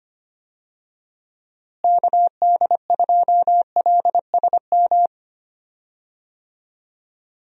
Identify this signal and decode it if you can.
Morse code, heard as a keyed tone, about 25 wpm, 700 Hz. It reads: KD2LSM
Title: USA Call Signs - 6 Characters - Repeated 3x - 25wpm